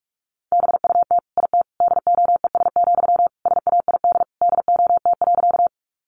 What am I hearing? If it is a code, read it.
Morse code, heard as a keyed tone, about 44 wpm, 700 Hz. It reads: BUT IT DOES, SAID DOT.
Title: Sentences from Top 1500 Words 45wpm